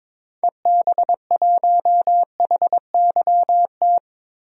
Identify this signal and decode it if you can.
Morse code, heard as a keyed tone, about 21 wpm, 700 Hz. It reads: EB1HYT
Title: DXCC-100 Call Signs - 6 Characters - Repeated 3x - 22wpm